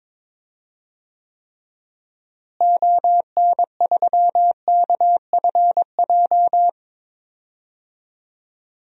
Morse code, heard as a keyed tone, about 22 wpm, 700 Hz. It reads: ON3KFJ
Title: DXCC-100 Call Signs - 6 Characters - Repeated 3x - 22wpm